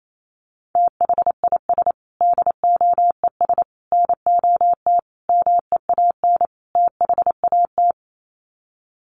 Morse code, heard as a keyed tone, about 28 wpm, 700 Hz. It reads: THIS DOES NOT MEAN THAT
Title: Sets of 5 Words Encore2 28wpm